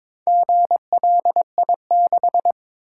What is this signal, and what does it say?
Morse code, heard as a keyed tone, about 22 wpm, 700 Hz. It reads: GLI6
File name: practice / 22wpm / Four Letter-Number Combinations - 22wpm